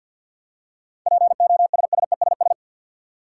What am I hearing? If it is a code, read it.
Morse code, heard as a keyed tone, about 49 wpm, 700 Hz. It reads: POSSESS
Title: Seven Letter Words - 50wpm